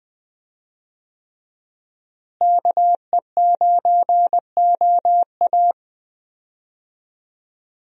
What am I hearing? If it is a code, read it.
Morse code, heard as a keyed tone, about 21 wpm, 700 Hz. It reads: KE9OA
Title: USA Call Signs - 5 Characters - Repeated 3x - 20wpm